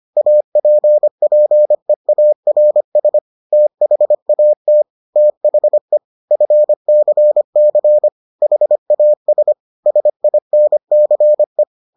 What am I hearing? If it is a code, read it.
Morse code, heard as a keyed tone, about 25 wpm, 600 Hz. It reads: APPEARS THAT THE FCC HAS SINCE